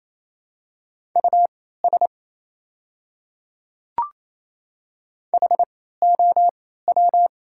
Morse code, heard as a keyed tone, about 28 wpm, spaced as for 16 wpm, 700 Hz. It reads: US HOW
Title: 3x Character Spacing - Top 200 Words - 28wpm